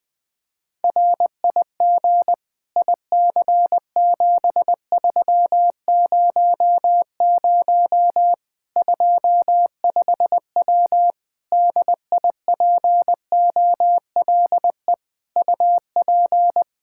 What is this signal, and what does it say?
Morse code, heard as a keyed tone, about 20 wpm, 700 Hz. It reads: RIG IC7300 25W DIPOLE UP
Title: K1USN SST QSO Practice 20wpm